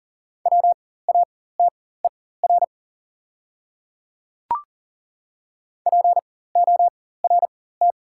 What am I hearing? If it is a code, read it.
Morse code, heard as a keyed tone, about 40 wpm, spaced as for 19 wpm, 690 Hz. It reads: WATER PORT
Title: 4x Character Spacing - Top 200 Words - 40wpm